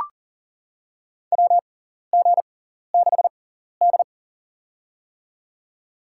Morse code, heard as a keyed tone, about 40 wpm, spaced as for 14 wpm, 700 Hz. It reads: WG6D
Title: USA Call Signs - 6x Character Spacing - 40wpm